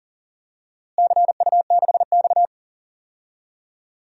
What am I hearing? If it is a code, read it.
Morse code, heard as a keyed tone, about 40 wpm, 700 Hz. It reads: CU6X
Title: DXCC-100 Call Signs - 4 Characters - Repeated 3x - 40wpm